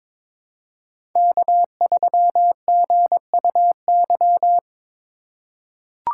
Morse code, heard as a keyed tone, about 21 wpm, 700 Hz. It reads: K3GUY
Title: USA Call Signs - 3x Character Spacing - 22wpm